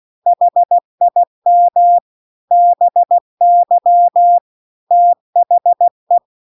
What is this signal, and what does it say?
Morse code, heard as a keyed tone, about 16 wpm, 700 Hz. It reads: HIM BY THE
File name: CW-The-Big-Twoheaded-River-2-16WPM